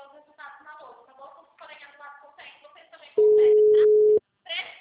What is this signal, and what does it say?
Ringback tone (Europe, 425 Hz), with speech.